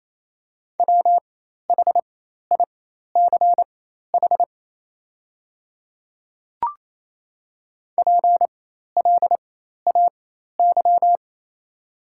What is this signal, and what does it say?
Morse code, heard as a keyed tone, about 27 wpm, spaced as for 13 wpm, 700 Hz. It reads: WHICH PLAY
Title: 4x Character Spacing - Top 500 Words - 28wpm